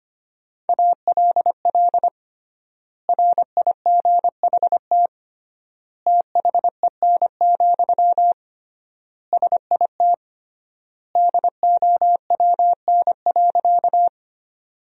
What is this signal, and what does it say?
Morse code, heard as a keyed tone, about 25 wpm, 700 Hz. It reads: ALL RIGHT THEN, SIT DOWN.
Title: 3x Word-Spacing - Sentences from Top 500 Words - 25wpm